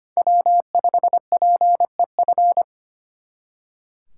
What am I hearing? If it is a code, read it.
Morse code, heard as a keyed tone, about 32 wpm, 700 Hz. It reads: W5PEF